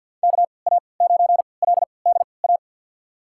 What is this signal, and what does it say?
Morse code, heard as a keyed tone, about 52 wpm, spaced as for 29 wpm, 700 Hz. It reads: KA9LDA
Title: USA Call Signs - 6 Characters - 3x Character Spacing - 50wpm